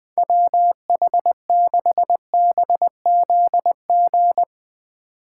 Morse code, heard as a keyed tone, about 20 wpm, 700 Hz. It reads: WH6BZG
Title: Speed Racing - USA Call Signs 17wpm